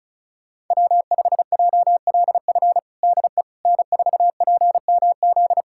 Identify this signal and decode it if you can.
Morse code, heard as a keyed tone, about 36 wpm, 700 Hz. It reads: W5JLF DE N4PMZ
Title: USA Callsign DE Callsign - 5 Characters - 35wpm